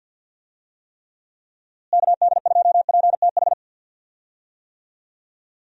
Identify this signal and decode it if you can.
Morse code, heard as a keyed tone, about 50 wpm, 700 Hz. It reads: KD2PTH
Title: USA Call Signs - 6 Characters - Repeated 3x - 50wpm